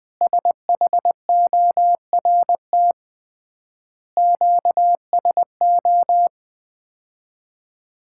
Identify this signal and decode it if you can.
Morse code, heard as a keyed tone, about 20 wpm, 700 Hz. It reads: SHORT QSO
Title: Common QSO Phrases - 3x Word-Spacing - 20wpm